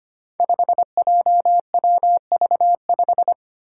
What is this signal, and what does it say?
Morse code, heard as a keyed tone, about 25 wpm, 700 Hz. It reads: 5JWV5
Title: Five Challenging Letter-Number Combinations 25wpm